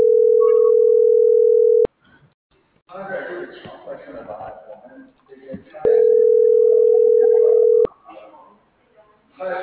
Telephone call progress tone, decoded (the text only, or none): ringback tone